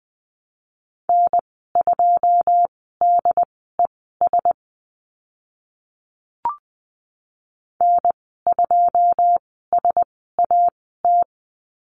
Morse code, heard as a keyed tone, about 20 wpm, spaced as for 14 wpm, 700 Hz. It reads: N2DES N2SAT
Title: 088 Lesson 15g - Number 5 - Call Signs 20wpm